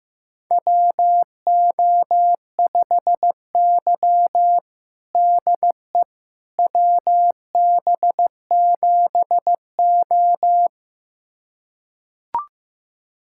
Morse code, heard as a keyed tone, about 15 wpm, 700 Hz. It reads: WO5Y DE WB7O
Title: USA Callsign DE Callsign - 4 Characters - 15wpm